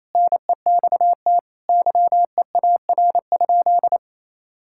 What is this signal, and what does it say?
Morse code, heard as a keyed tone, about 28 wpm, 700 Hz. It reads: NEXT YEAR?
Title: Sentences from Top 1000 Words - Review Part 5 - 28wpm